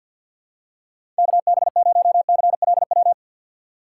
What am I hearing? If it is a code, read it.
Morse code, heard as a keyed tone, about 51 wpm, 700 Hz. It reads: KB0CLW
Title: USA Call Signs - 3x Character Spacing - 50wpm